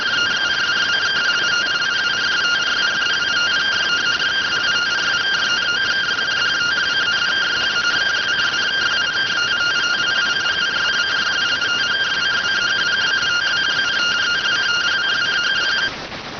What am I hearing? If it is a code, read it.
Two-tone teleprinter signal, about 170 Hz shift.